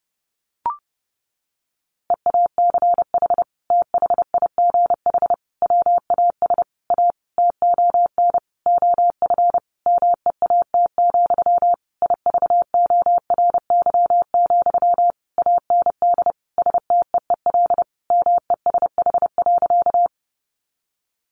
Morse code, heard as a keyed tone, about 30 wpm, 700 Hz. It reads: EACH THIGH WAS A TON OF MEAT, IVORY, AND STEEL MESH.
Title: A Sound Of Thunder 30wpm